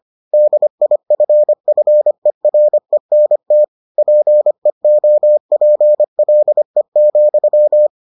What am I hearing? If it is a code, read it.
Morse code, heard as a keyed tone, about 25 wpm, 600 Hz. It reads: DIFFERENT PEOPLE,